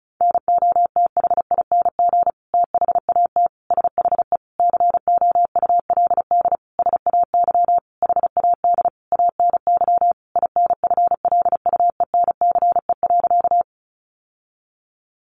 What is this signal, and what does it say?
Morse code, heard as a keyed tone, about 34 wpm, 700 Hz. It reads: NOTHING THAT SHE COULD SAY HAD ANY INFLUENCE.
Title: Sentences from Top 1100 Words 35wpm